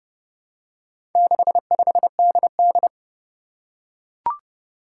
Morse code, heard as a keyed tone, about 30 wpm, 700 Hz. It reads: 65DD
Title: Four Challenging Letter-Number Combinations 30wpm